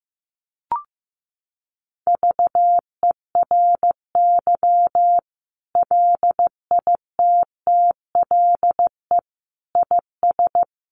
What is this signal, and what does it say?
Morse code, heard as a keyed tone, about 15 wpm, 700 Hz. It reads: VERY LITTLE IS
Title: Sentences from Top 1500 Words 15wpm